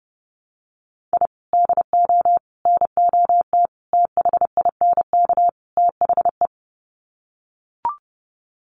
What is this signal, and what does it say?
Morse code, heard as a keyed tone, about 30 wpm, 700 Hz. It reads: I DO NOT THINK THE
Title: Sets of 5 Words 30wpm